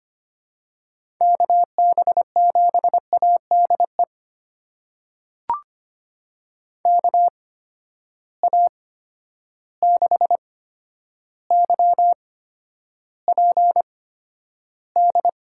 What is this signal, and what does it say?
Morse code, heard as a keyed tone, about 25 wpm, spaced as for 11 wpm, 700 Hz. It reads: KB7ADE KA6YPD
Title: USA Call Signs - 6 Characters - 8x Character Spacing - 25wpm